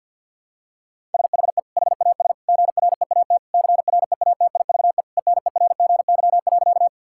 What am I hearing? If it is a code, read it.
Morse code, heard as a keyed tone, about 50 wpm, 700 Hz. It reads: SHE HAS GREAT CREATIVE ENERGY.
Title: Sentences from Top 1000 Words - Review Part 6 - 50wpm